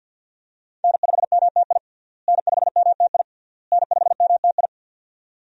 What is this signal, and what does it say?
Morse code, heard as a keyed tone, about 45 wpm, 700 Hz. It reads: N5MTI N5MTI N5MTI
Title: USA Call Signs - 5 Characters - Repeated 3x - 50wpm